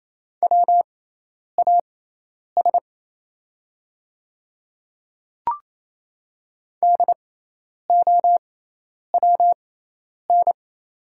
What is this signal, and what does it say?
Morse code, heard as a keyed tone, about 28 wpm, spaced as for 10 wpm, 700 Hz. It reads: WAS DOWN
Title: 6x Character Spacing - Top 100 Words - 28wpm